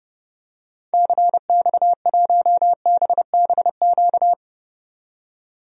Morse code, heard as a keyed tone, about 30 wpm, 700 Hz. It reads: CX1BBQ
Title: DXCC-100 Call Signs - 6 Characters - 30wpm